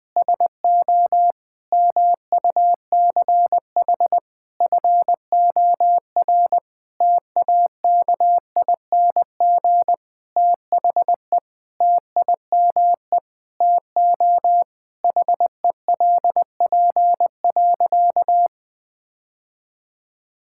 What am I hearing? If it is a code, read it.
Morse code, heard as a keyed tone, about 21 wpm, 700 Hz. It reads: SO MUCH FOR TAKING THE TIME TO HELP.
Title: Sentences with 9-Word Phrases - 20wpm